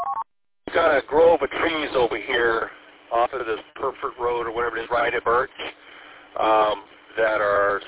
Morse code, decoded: none

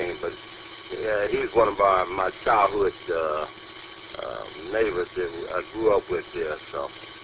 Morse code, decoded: none

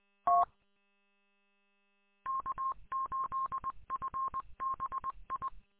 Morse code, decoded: T K8FBI

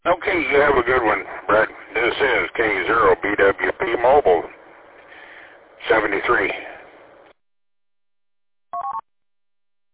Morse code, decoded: none

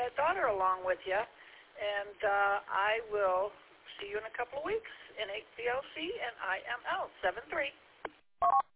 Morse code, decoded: none

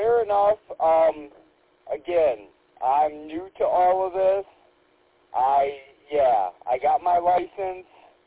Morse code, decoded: none

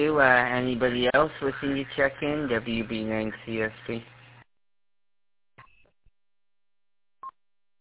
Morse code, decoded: none

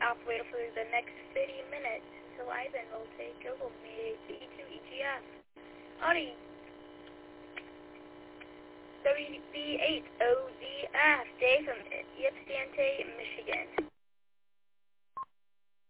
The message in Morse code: none

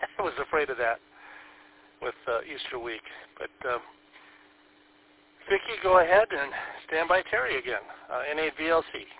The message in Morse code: none

